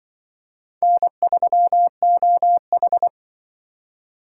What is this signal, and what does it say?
Morse code, heard as a keyed tone, about 24 wpm, 700 Hz. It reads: N3OH
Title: Speed Racing - USA 1x2 Call Signs 20wpm